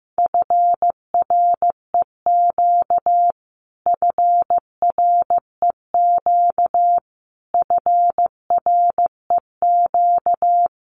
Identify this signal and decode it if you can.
Morse code, heard as a keyed tone, about 15 wpm, 700 Hz. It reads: FREQ FREQ FREQ
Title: Abbreviations Repeated 3x - 15wpm